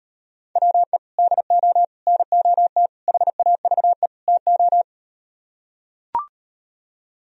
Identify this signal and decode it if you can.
Morse code, heard as a keyed tone, about 38 wpm, 700 Hz. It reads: WE DO NOT HAVE TO